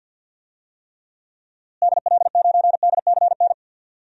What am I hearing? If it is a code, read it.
Morse code, heard as a keyed tone, about 52 wpm, 700 Hz. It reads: DL9DCN
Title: DXCC-100 Call Signs - 6 Characters - Repeated 3x - 50wpm